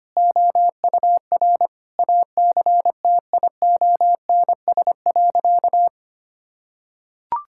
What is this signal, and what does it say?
Morse code, heard as a keyed tone, about 25 wpm, 700 Hz. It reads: OUR ACTIONS.
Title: Sentences from Top 1000 Words - Review Part 6 - 25wpm